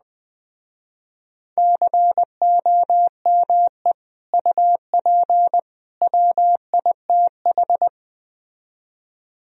Morse code, heard as a keyed tone, about 20 wpm, 700 Hz. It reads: COME UP WITH
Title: Speed-Racing - Sets of 3 Words 17wpm